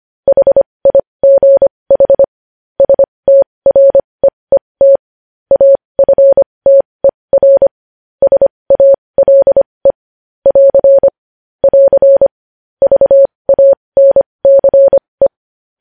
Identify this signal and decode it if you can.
Morse code, heard as a keyed tone, about 25 wpm, spaced as for 20 wpm, 550 Hz. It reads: HIGH STREET AFTER SALE <AR> <AR> VANCE
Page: news-headlines-in-morse-code-at-20-wpm